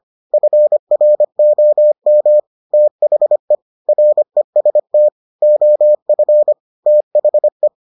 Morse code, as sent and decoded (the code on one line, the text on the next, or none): ..-. .-. --- -- / - .... / .-. ... - / --- ..-. / - .... .
FROM THE REST OF THE